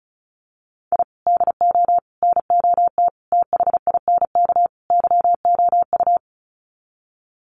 Morse code, heard as a keyed tone, about 34 wpm, 700 Hz. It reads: I DO NOT THINK YOU
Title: Sets of 5 Words 35wpm